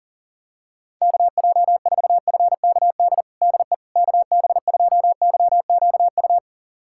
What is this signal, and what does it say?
Morse code, heard as a keyed tone, about 37 wpm, 700 Hz. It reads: KJ4FKD DE KB2YQU
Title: USA Callsign DE Callsign 40wpm